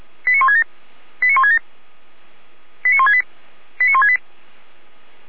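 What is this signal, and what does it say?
Five-tone selective call (ZVEI), heard as quick runs of five tones, tones of about 80 ms. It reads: 89167 89167 89168 89168
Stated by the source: ZVEI five-tone selective call received via SDR.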